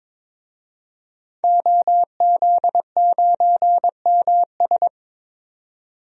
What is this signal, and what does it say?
Morse code, heard as a keyed tone, about 22 wpm, 700 Hz. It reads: OZ9MS
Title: DXCC-100 Call Signs - 5 Characters - 22wpm